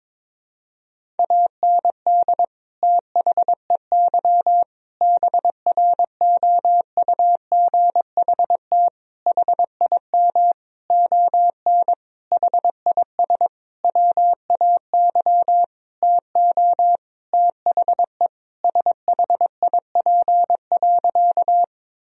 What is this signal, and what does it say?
Morse code, heard as a keyed tone, about 22 wpm, 700 Hz. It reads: AND THEY BROUGHT HIM ON HIS WAY TO THE SHIP.
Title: Sentences from Top 1000 Words - Review Part 2 - 22wpm